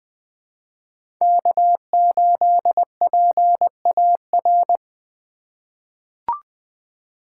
Morse code, heard as a keyed tone, about 20 wpm, 700 Hz. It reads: K8PAR
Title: USA Call Signs - 5 Characters - 3x Character Spacing - 20wpm